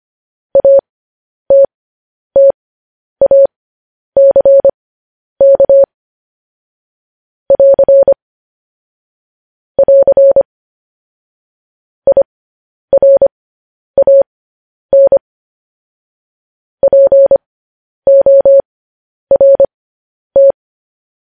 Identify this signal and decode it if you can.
Morse code, heard as a keyed tone, about 25 wpm, spaced as for 10 wpm, 550 Hz. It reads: ATTACK <AR> <AR> IRAN PORT
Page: news headlines in morse code at 10 wpm